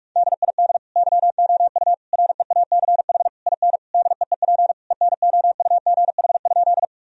Morse code, heard as a keyed tone, about 41 wpm, 700 Hz. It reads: DID YOU REACH IN DEEP ENOUGH?
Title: Sentences from Top 1000 Words - Review Part 3 - 45wpm